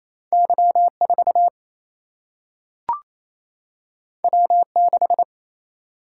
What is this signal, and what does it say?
Morse code, heard as a keyed tone, about 28 wpm, 700 Hz. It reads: Y4 W6
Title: Single-Letter Single-Number 28wpm